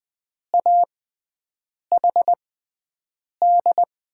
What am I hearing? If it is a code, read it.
Morse code, heard as a keyed tone, about 20 wpm, spaced as for 7 wpm, 700 Hz. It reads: AHD
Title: USA Call Signs - 6 Characters - 6x Character Spacing - 20wpm